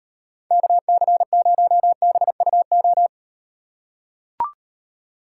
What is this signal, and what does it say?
Morse code, heard as a keyed tone, about 38 wpm, 700 Hz. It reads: KC0BUO